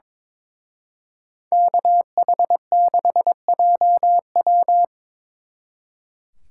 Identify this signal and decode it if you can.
Morse code, heard as a keyed tone, about 26 wpm, 700 Hz. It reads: KH6JW
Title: Speed Racing - USA 2x2 Call Signs 22wpm